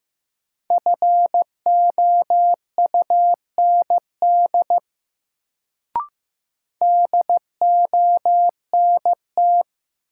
Morse code, heard as a keyed tone, about 15 wpm, 700 Hz. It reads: FOUND DONT
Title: Sentences from Top 300 Words - New Word Drill - 15wpm